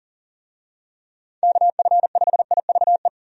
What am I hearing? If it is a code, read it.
Morse code, heard as a keyed tone, about 40 wpm, 700 Hz. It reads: KF5IVE
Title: USA Call Signs - 6 Characters - 40wpm